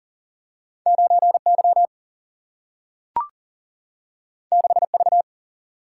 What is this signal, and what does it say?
Morse code, heard as a keyed tone, about 40 wpm, 700 Hz. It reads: 9Y 6V
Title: Single-Letter Single-Number 40wpm